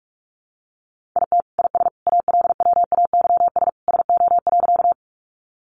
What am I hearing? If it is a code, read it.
Morse code, heard as a keyed tone, about 44 wpm, 700 Hz. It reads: IT IS ALWAYS SO.
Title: Sentences from Top 300 Words 45wpm